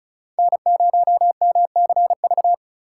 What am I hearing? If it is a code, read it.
Morse code, heard as a keyed tone, about 36 wpm, 700 Hz. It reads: N0MCV
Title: USA Call Signs - 5 Characters - 2x Character Spacing - 35wpm